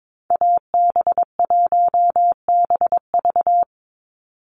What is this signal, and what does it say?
Morse code, heard as a keyed tone, about 22 wpm, 700 Hz. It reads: AB1BV
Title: USA Call Signs 22wpm